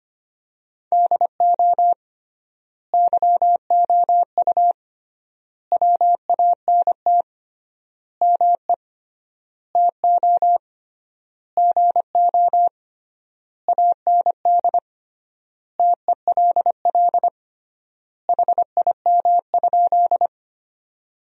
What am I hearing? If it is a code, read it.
Morse code, heard as a keyed tone, about 25 wpm, 700 Hz. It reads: DO YOU WANT ME TO GO AND TELL HIM?
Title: 3x Word-Spacing - Sentences from Top 200 Words - 25wpm